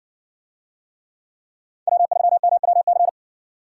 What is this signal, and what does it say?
Morse code, heard as a keyed tone, about 60 wpm, 700 Hz. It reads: W3MWC